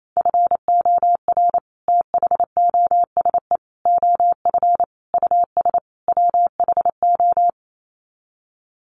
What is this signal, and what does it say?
Morse code, heard as a keyed tone, about 28 wpm, 700 Hz. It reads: FOR THOSE OF US WHO
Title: Sets of 5 Words Encore 28wpm